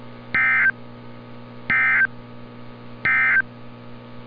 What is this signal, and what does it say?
Emergency Alert System (SAME) tones, heard as harsh buzzing bursts. It is heard from 0.2 to 3.5 s.
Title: NOAA Weather Radio - Altoona, PA